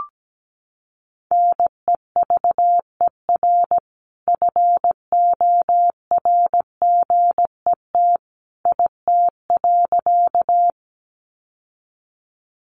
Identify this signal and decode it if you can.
Morse code, heard as a keyed tone, about 17 wpm, 700 Hz. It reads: NEVER FORGET IT.
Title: Sentences from Top 1200 Words 17wpm